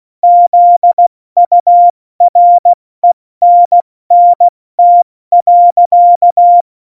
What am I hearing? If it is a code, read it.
Morse code, heard as a keyed tone, about 16 wpm, 700 Hz. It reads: ZURENNT.
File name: Morsekurs / Witze / Witz 26 Mai, 2020 70 F80